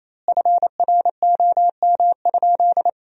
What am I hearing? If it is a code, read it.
Morse code, heard as a keyed tone, about 27 wpm, 700 Hz. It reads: FROM?
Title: 3x Word-Spacing - Sentences from Top 200 Words - 28wpm